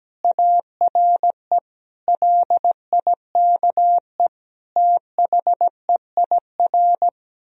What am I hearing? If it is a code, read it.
Morse code, heard as a keyed tone, about 17 wpm, 700 Hz. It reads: ARE LIKE THEIR
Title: Sentences from Top 1000 Words - Review Part 3 - 17wpm